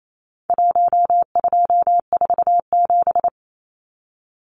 Morse code, heard as a keyed tone, about 28 wpm, 700 Hz. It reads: 1247
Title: Numbers - Contest Style - 4-Digit 28wpm